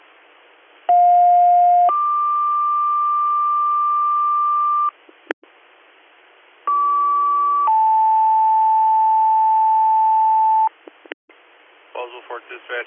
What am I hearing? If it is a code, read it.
Two-tone page, heard as a one-second tone, then a three-second tone, about 710 then 1180 Hz, about 1150 then 880 Hz, with speech. It is heard from 0.9 to 10.7 s.